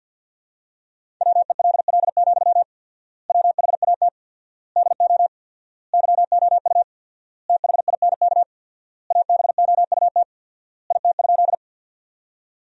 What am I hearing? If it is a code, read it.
Morse code, heard as a keyed tone, about 51 wpm, 700 Hz. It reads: WELL, WHAT DO YOU THINK ABOUT IT?
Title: 4x Word-Spacing - Sentences from Top 200 Words - 50wpm